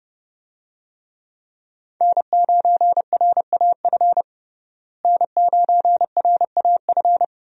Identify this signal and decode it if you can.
Morse code, heard as a keyed tone, about 29 wpm, 700 Hz. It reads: N9RAF N9RAF
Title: USA Call Signs - 5 Characters - Repeated 2x - 30wpm